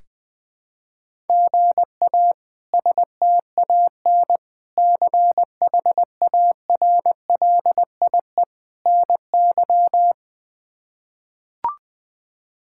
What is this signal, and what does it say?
Morse code, heard as a keyed tone, about 20 wpm, 700 Hz. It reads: GA STAN CHARLIE NY
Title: K1USN SST QSO Practice - 20wpm